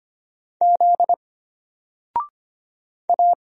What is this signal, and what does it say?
Morse code, heard as a keyed tone, about 25 wpm, 700 Hz. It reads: Z A
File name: Single-Letter-Number-25wpm